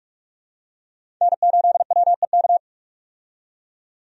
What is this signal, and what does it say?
Morse code, heard as a keyed tone, about 46 wpm, 700 Hz. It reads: N8WEK
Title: USA Call Signs - 5 Characters - 45wpm